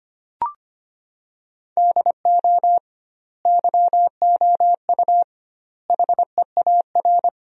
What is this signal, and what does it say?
Morse code, heard as a keyed tone, about 25 wpm, 700 Hz. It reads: DO YOU HEAR
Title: 2x Word-Spacing - Sentences from Top 300 Words - 25wpm